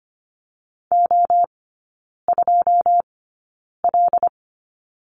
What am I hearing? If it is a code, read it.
Morse code, heard as a keyed tone, about 25 wpm, spaced as for 9 wpm, 690 Hz. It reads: O2L
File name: Lesson 14e - Number 2 - Three Characters 25wpm